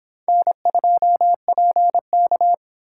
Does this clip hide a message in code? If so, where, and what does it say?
Yes: Morse code from 0.3 to 2.5 s, heard as a keyed tone, about 26 wpm, 700 Hz. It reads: N2PK